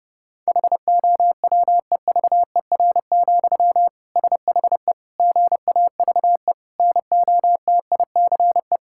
Morse code, heard as a keyed tone, about 30 wpm, 700 Hz. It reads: HOWEVER, SHE GAVE NOTICE
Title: Sentences from Top 1000 Words - Review Part 6 - 30wpm